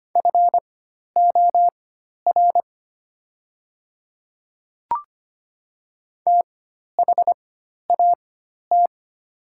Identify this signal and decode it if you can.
Morse code, heard as a keyed tone, about 25 wpm, spaced as for 12 wpm, 690 Hz. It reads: FOR THAT
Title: 4x Character Spacing - Top 100 Words - 25wpm